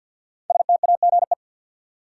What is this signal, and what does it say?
Morse code, heard as a keyed tone, about 47 wpm, 700 Hz. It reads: STAGE
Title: Five Letter Words - 50wpm